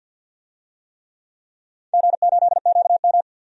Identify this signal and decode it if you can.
Morse code, heard as a keyed tone, about 48 wpm, 700 Hz. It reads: G8QM
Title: DXCC-100 Call Signs - 4 Characters - 50wpm